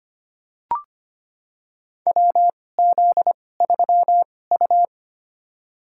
Morse code, heard as a keyed tone, about 25 wpm, spaced as for 18 wpm, 700 Hz. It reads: WZ3U
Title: USA Call Signs - 2x Character Spacing - 25wpm